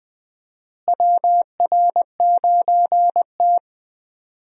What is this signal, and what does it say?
Morse code, heard as a keyed tone, about 20 wpm, 700 Hz. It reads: WR9T
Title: USA Call Signs - 4 Characters - Repeated 2x - 20wpm